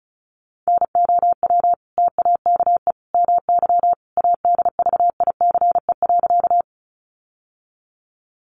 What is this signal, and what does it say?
Morse code, heard as a keyed tone, about 34 wpm, 700 Hz. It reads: NOW TAKE MY ADVICE.
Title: Sentences from Top 1100 Words 35wpm